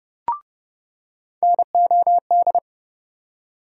Morse code, heard as a keyed tone, about 31 wpm, 700 Hz. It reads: NOD
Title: USA Call Sign Postfixes - 3 Letters 30wpm